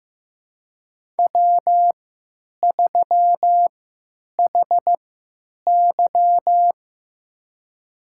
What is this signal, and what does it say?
Morse code, heard as a keyed tone, about 15 wpm, spaced as for 9 wpm, 700 Hz. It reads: W3HY
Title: USA Call Signs - 3x Character Spacing - 15wpm